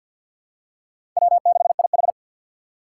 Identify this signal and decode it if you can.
Morse code, heard as a keyed tone, about 47 wpm, 700 Hz. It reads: W6IH
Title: USA Call Signs - 4 Characters - 6x Character Spacing - 50wpm